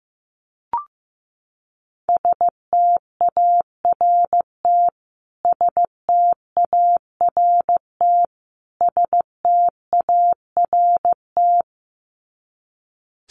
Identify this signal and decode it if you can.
Morse code, heard as a keyed tone, about 15 wpm, 700 Hz. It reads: START START START
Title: Top 500 Words Repeated 15wpm